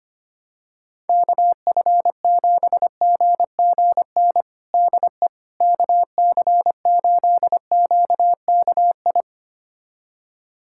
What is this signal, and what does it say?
Morse code, heard as a keyed tone, about 25 wpm, 700 Hz. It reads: KF7GGN DE KC8QKI
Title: USA Callsign DE Callsign 25wpm